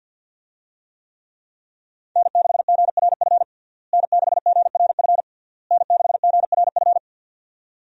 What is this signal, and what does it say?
Morse code, heard as a keyed tone, about 47 wpm, 700 Hz. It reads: N6GRF N6GRF N6GRF
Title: USA Call Signs - 5 Characters - Repeated 3x - 50wpm